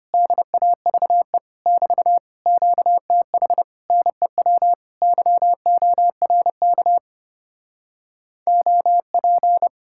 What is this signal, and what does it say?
Morse code, heard as a keyed tone, about 33 wpm, 700 Hz. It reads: DAVE <BT> QTH NEW YORK OP